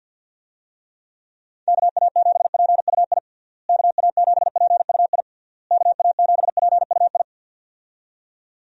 Morse code, heard as a keyed tone, about 49 wpm, 700 Hz. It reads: KA7PUI KA7PUI KA7PUI
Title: USA Call Signs - 6 Characters - Repeated 3x - 50wpm